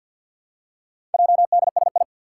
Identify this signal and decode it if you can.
Morse code, heard as a keyed tone, about 51 wpm, 700 Hz. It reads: JDSI
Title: Four Challenging Letter-Number Combinations 50wpm